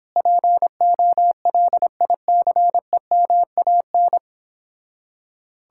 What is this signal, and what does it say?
Morse code, heard as a keyed tone, about 29 wpm, 700 Hz. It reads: POLICEMAN